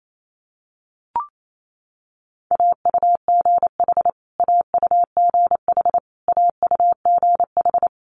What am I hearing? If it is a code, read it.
Morse code, heard as a keyed tone, about 28 wpm, 700 Hz. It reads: AUGH AUGH AUGH
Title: Multi-letter Phonemes Repeated 3x 28wpm